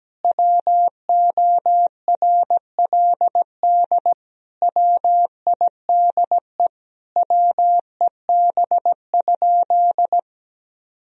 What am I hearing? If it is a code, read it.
Morse code, heard as a keyed tone, about 17 wpm, 700 Hz. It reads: WORLD WIDE WEB?
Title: Sentences from Top 1000 Words - Review Part 4 - 17wpm